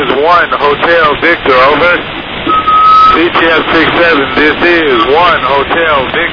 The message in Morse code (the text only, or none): none